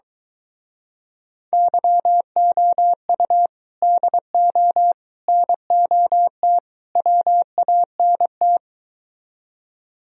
Morse code, heard as a keyed tone, about 23 wpm, 700 Hz. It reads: YOU DO NOT WANT